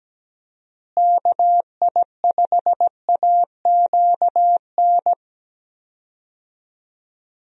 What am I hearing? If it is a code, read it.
Morse code, heard as a keyed tone, about 17 wpm, 700 Hz. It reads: KI5AQN
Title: USA Call Signs - 6 Characters - Repeated 2x - 17wpm